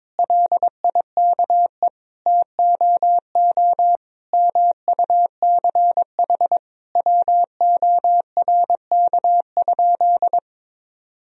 Morse code, heard as a keyed tone, about 22 wpm, 700 Hz. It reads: LIKE TOO MUCH WORK?
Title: Sentences from Top 1000 Words - Review Part 1 - 22wpm